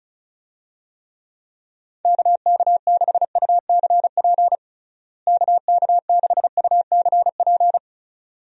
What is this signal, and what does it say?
Morse code, heard as a keyed tone, about 34 wpm, 700 Hz. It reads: KK6UCP KK6UCP
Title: USA Call Signs - 6 Characters - Repeated 2x - 35wpm